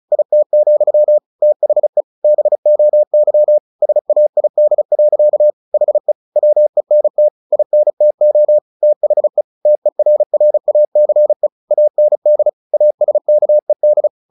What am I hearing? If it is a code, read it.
Morse code, heard as a keyed tone, about 34 wpm, 600 Hz. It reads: IT, THE BOY SAID. HE WENT INTO THE TERRACE AND ASKED
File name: the old man and the sea 2 35wpm